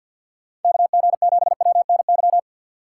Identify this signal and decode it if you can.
Morse code, heard as a keyed tone, about 52 wpm, 690 Hz. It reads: KG7WNY
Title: USA Call Signs - 6x Character Spacing - 50wpm